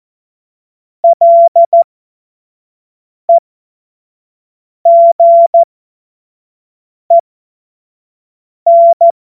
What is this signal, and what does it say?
Morse code, heard as a keyed tone, about 14 wpm, spaced as for 5 wpm, 680 Hz. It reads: LEGEN
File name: Morsekurs / Witze / Witz 05 März 2024 25 F70